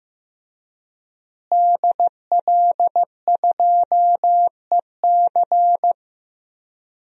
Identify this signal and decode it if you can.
Morse code, heard as a keyed tone, about 15 wpm, 700 Hz. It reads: DL2EC
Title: DXCC-100 Call Signs - 5 Characters - Repeated 3x - 15wpm